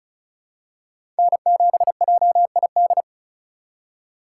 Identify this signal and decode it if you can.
Morse code, heard as a keyed tone, about 36 wpm, 700 Hz. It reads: N7JID
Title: USA Call Signs - 5 Characters - 2x Character Spacing - 35wpm